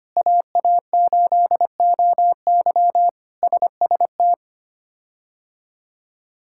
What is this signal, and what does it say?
Morse code, heard as a keyed tone, about 25 wpm, 700 Hz. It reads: AA8OY SST
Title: K1USN SST QSO Practice - 25wpm